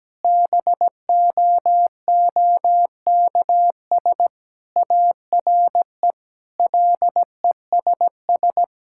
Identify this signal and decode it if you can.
Morse code, heard as a keyed tone, about 17 wpm, 700 Hz. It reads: BOOKS ARE LESS
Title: Sentences from Top 1000 Words - Review Part 6 - 17wpm